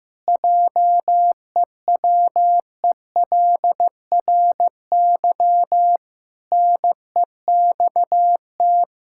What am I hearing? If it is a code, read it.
Morse code, heard as a keyed tone, about 15 wpm, 700 Hz. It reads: JEWELRY NEXT
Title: Pangram Challenge v2 - 15wpm